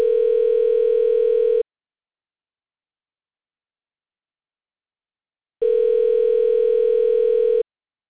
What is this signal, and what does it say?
Ringback tone (North America), 440 + 480 Hz.